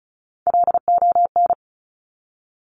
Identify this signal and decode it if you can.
Morse code, heard as a keyed tone, about 36 wpm, 700 Hz. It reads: LON
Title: Top Three Letter Combinations in Words 35wpm